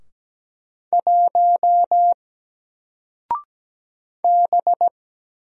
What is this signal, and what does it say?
Morse code, heard as a keyed tone, about 17 wpm, 700 Hz. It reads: 1 B
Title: Single Letter-Number 17wpm